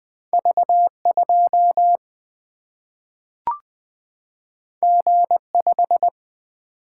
Morse code, heard as a keyed tone, about 20 wpm, 700 Hz. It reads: V2 G5
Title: Single-Letter Single-Number 20wpm